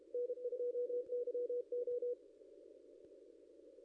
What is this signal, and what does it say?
Morse code, heard as a keyed tone, about 25 wpm, 500 Hz. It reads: N2YO
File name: contests / ss-cw-k5tr-op-n5rz -0700